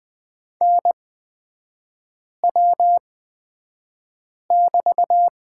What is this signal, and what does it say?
Morse code, heard as a keyed tone, about 20 wpm, spaced as for 5 wpm, 700 Hz. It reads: NW<BT>
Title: 250 Lesson 40e - BK - Three Characters 20wpm